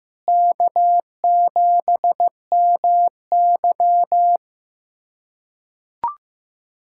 Morse code, heard as a keyed tone, about 15 wpm, 700 Hz. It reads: K7MY
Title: K1USN SST QSO Practice - 15wpm